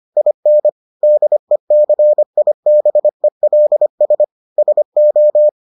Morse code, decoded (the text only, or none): IN DECIBELS SO